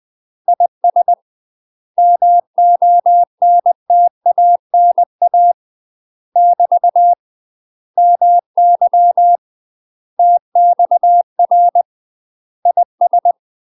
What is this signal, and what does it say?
Morse code, heard as a keyed tone, about 20 wpm, 700 Hz. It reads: IS MONTANA <BT> MY TXR IS
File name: UK MASTER QSOs TRAINING / 20WPM / UK MASTER QSOs 0067